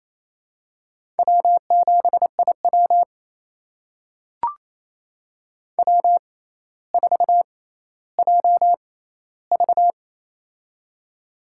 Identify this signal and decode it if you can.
Morse code, heard as a keyed tone, about 28 wpm, spaced as for 14 wpm, 700 Hz. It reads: W7IW W4JV
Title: USA Call Signs - 4 Characters - 6x Character Spacing - 28wpm